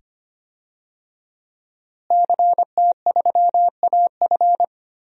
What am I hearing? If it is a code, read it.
Morse code, heard as a keyed tone, about 25 wpm, 700 Hz. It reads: CT3AF
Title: DXCC-100 Call Signs - 5 Characters - Repeated 2x - 25wpm